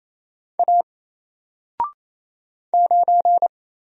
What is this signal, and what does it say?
Morse code, heard as a keyed tone, about 28 wpm, 700 Hz. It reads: A 9
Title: Single Letter-Number 28wpm